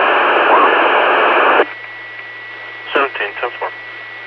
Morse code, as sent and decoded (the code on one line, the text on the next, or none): none